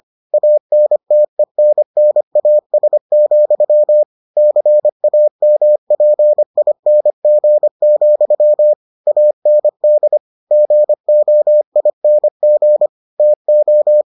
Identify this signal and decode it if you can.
Morse code, heard as a keyed tone, about 25 wpm, 600 Hz. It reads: ANTENNAS, CAMPING, AND GOING TO